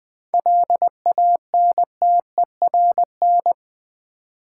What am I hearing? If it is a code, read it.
Morse code, heard as a keyed tone, about 20 wpm, 700 Hz. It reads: LANTERN
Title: Easy Words with Hints - 20wpm